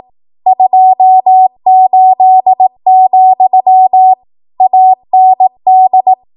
Morse code, heard as a keyed tone, about 18 wpm, 750 Hz. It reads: 28, AND
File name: Morse / 18WPM